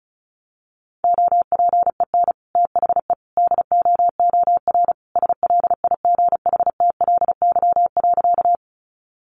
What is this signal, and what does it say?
Morse code, heard as a keyed tone, about 34 wpm, 700 Hz. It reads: OPEN THE DOOR SLIGHTLY.
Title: Sentences from Top 1300 Words 35wpm